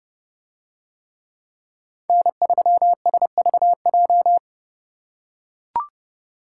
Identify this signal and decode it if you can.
Morse code, heard as a keyed tone, about 29 wpm, 700 Hz. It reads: N3SVJ